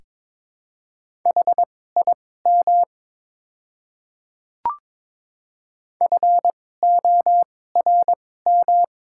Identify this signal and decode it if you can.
Morse code, heard as a keyed tone, about 22 wpm, spaced as for 16 wpm, 700 Hz. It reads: HIM FORM
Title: 2x Character Spacing - Top 200 Words - 22wpm